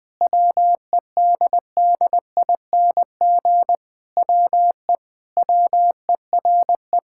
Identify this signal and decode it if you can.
Morse code, heard as a keyed tone, about 21 wpm, 700 Hz. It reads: WEDDING WE WERE
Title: Sentences with 7-Word Phrases - 20wpm